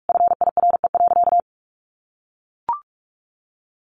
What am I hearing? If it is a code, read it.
Morse code, heard as a keyed tone, about 44 wpm, 700 Hz. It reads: FIRE.
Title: Easy Phrases 45wpm